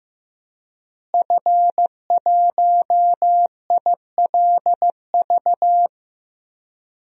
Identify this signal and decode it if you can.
Morse code, heard as a keyed tone, about 15 wpm, 700 Hz. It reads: F1ILV
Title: DXCC-100 Call Signs - 5 Characters - Repeated 3x - 15wpm